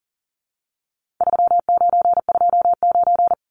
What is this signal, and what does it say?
Morse code, heard as a keyed tone, about 40 wpm, 700 Hz. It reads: 3929
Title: Numbers - 4-Digit 40wpm